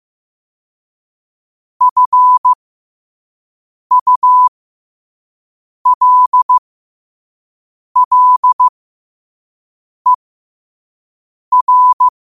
Morse code, heard as a keyed tone, about 15 wpm, spaced as for 5 wpm, 1000 Hz. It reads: FULLER
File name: qotdfiles / 5wpm